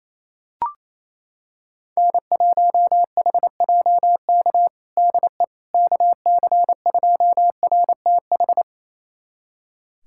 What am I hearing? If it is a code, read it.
Morse code, heard as a keyed tone, about 28 wpm, 700 Hz. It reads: N1HJK DE KC2RTH
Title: USA Callsign DE Callsign 28wpm